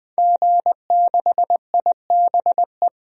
Morse code, heard as a keyed tone, about 20 wpm, 700 Hz. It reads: G6IBE
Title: DXCC-100 Call Signs - 5 Characters - Repeated 2x - 20wpm